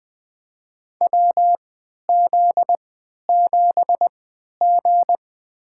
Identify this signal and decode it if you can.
Morse code, heard as a keyed tone, about 20 wpm, spaced as for 11 wpm, 700 Hz. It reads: WZ7G
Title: USA Call Signs - 3x Character Spacing - 20wpm